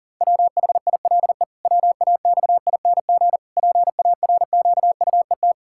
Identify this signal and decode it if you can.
Morse code, heard as a keyed tone, about 37 wpm, 700 Hz. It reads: WHILE WAXING PARQUET
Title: Pangram Challenge v2 - 40wpm